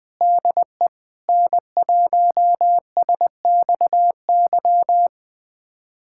Morse code, heard as a keyed tone, about 20 wpm, 700 Hz. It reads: DE N1SXY
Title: USA Callsign DE Callsign - 5 Characters - 20wpm